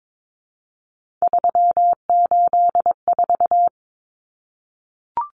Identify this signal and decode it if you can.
Morse code, heard as a keyed tone, about 22 wpm, 700 Hz. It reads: 384
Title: Common Cut Numbers - 3-Digit 22wpm